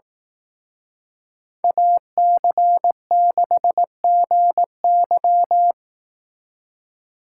Morse code, heard as a keyed tone, about 18 wpm, 700 Hz. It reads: AC6GY
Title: Speed Racing - USA 2x2 Call Signs 15wpm